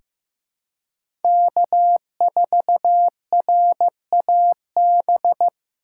Morse code, heard as a keyed tone, about 15 wpm, 700 Hz. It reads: K4RAB
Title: K1USN SST QSO Practice - 15wpm